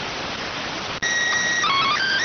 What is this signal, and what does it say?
SSTV (slow-scan television), PD 290.
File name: findings / CT AXIAL -377.gen-10 rx